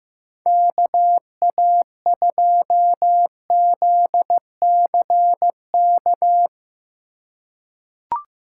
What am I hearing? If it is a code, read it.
Morse code, heard as a keyed tone, about 15 wpm, 700 Hz. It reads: KA2ZCK
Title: USA Call Signs - 4x Character Spacing - 15wpm